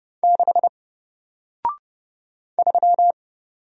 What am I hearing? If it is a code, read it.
Morse code, heard as a keyed tone, about 30 wpm, 700 Hz. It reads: 6 3
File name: Single-Letter-Number-30wpm